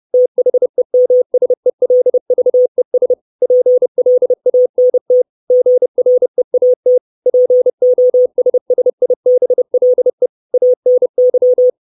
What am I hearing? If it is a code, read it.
Morse code, heard as a keyed tone, about 30 wpm, 500 Hz. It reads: THEMSELVES PLANT GREAT POSSIBLE ANY
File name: 632 Words 30WPM